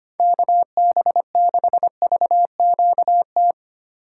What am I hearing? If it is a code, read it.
Morse code, heard as a keyed tone, about 25 wpm, 700 Hz. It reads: KB6VQT
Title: USA Call Signs - 6 Characters - 3x Character Spacing - 25wpm